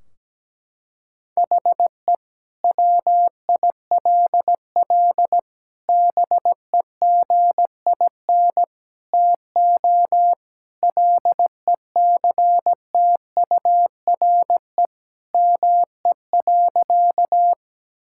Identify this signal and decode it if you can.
Morse code, heard as a keyed tone, about 17 wpm, 700 Hz. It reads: HE WILL BEGIN TO LECTURE ME.
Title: Sentences from Top 1600 Words 17wpm